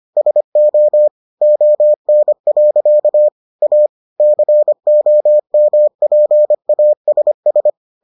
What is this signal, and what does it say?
Morse code, heard as a keyed tone, about 25 wpm, 600 Hz. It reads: SO ON. A COMPASS